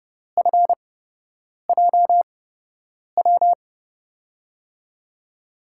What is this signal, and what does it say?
Morse code, heard as a keyed tone, about 30 wpm, spaced as for 8 wpm, 700 Hz. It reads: FJW
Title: USA Call Signs - 6 Characters - 8x Character Spacing - 30wpm